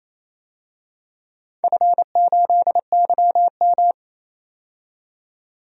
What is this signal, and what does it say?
Morse code, heard as a keyed tone, about 28 wpm, 700 Hz. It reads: F8YM
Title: DXCC-100 Call Signs - 4 Characters - 28wpm